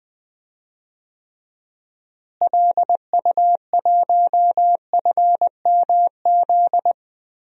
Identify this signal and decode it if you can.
Morse code, heard as a keyed tone, about 21 wpm, 700 Hz. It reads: LU1FMZ
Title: DXCC-100 Call Signs - 6 Characters - Repeated 3x - 20wpm